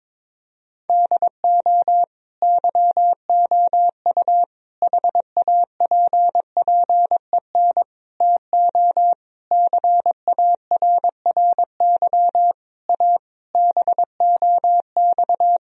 Morse code, heard as a keyed tone, about 22 wpm, 700 Hz. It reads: DO YOU HAPPEN TO CARRY A BOX
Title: Sentences from Top 1000 Words - Review Part 2 - 22wpm